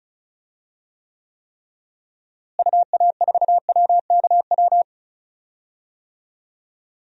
Morse code, heard as a keyed tone, about 36 wpm, 700 Hz. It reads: UA4WKW
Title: DXCC-100 Call Signs - 6 Characters - Repeated 3x - 35wpm